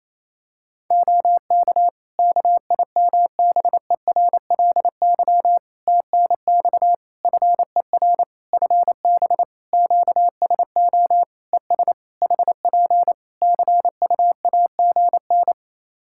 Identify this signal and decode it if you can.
Morse code, heard as a keyed tone, about 27 wpm, 700 Hz. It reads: OK KIMBERLY TNX FER FB QSO ES HP CUAGN
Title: Practice QSO - Exchange 4 - Ending - 28wpm